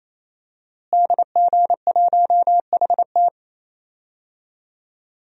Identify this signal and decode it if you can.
Morse code, heard as a keyed tone, about 28 wpm, 700 Hz. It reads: DG1HT
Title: DXCC-100 Call Signs - 5 Characters - Repeated 3x - 28wpm